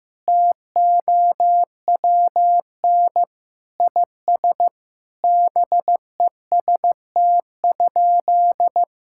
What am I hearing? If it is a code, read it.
Morse code, heard as a keyed tone, about 15 wpm, 700 Hz. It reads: TOWN IS BEST?
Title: Sentences from Top 1000 Words - Review Part 5 - 15wpm